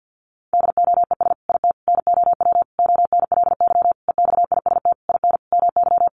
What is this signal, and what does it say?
Morse code, heard as a keyed tone, about 50 wpm, 700 Hz. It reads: DOES IT NOW ONLY EXIST IN MY
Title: Sentences from Top 1300 Words 50wpm